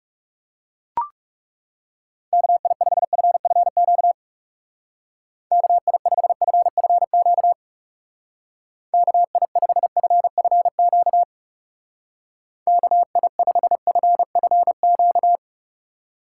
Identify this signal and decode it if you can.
Morse code, heard as a keyed tone, about 36 wpm, 700 Hz. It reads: KI5FFQ KI5FFQ KI5FFQ KI5FFQ
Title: Speed Racing - USA 2x3 Call Signs 30wpm